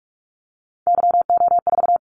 Morse code, heard as a keyed tone, about 46 wpm, 700 Hz. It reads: YO4
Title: Callsign Prefixes for Top 25 DXCC - 45wpm